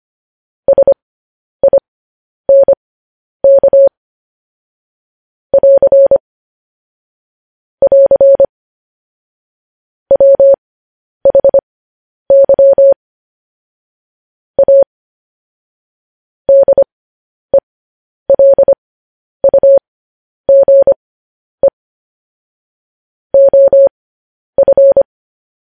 Morse code, heard as a keyed tone, about 25 wpm, spaced as for 10 wpm, 550 Hz. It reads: SINK <AR> <AR> WHY A DELUGE OF
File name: podcast / News headlines-1x-25x10